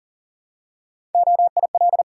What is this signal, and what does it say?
Morse code, heard as a keyed tone, about 39 wpm, 700 Hz. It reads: OIL
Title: Three Letter Words - 40wpm